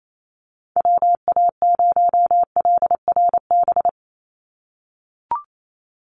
Morse code, heard as a keyed tone, about 28 wpm, 700 Hz. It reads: WA0LRB
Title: USA Call Signs 28wpm